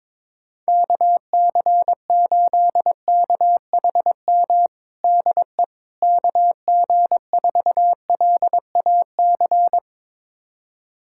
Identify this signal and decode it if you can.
Morse code, heard as a keyed tone, about 22 wpm, 700 Hz. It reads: KC8KHM DE KG4LAC
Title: USA Callsign DE Callsign - 6 Characters - 22wpm